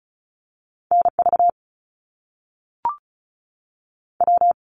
Morse code, heard as a keyed tone, about 36 wpm, 700 Hz. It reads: N V W
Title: Top Two Letter Combinations in Words 35wpm